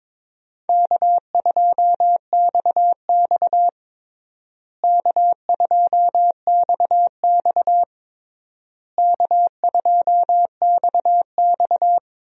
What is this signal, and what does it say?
Morse code, heard as a keyed tone, about 22 wpm, 700 Hz. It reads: K2XX K2XX K2XX
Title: USA Call Signs - 4 Characters - Repeated 3x - 22wpm